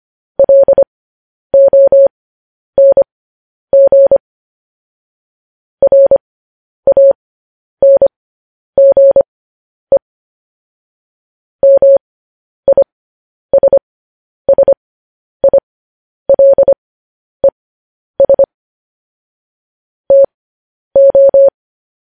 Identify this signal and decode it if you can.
Morse code, heard as a keyed tone, about 25 wpm, spaced as for 10 wpm, 550 Hz. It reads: LONG RANGE MISSILES TO